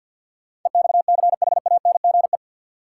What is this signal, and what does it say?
Morse code, heard as a keyed tone, about 49 wpm, 700 Hz. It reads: EXCHANGE
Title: Sentences from Top 1000 Words - Review Part 6 - New Word Drill - 50wpm